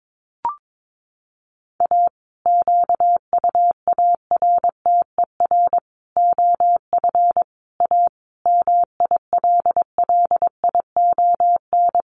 Morse code, heard as a keyed tone, about 22 wpm, 700 Hz. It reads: A QUARTER OF A MILLION
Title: Sets of 5 Words Encore2 22wpm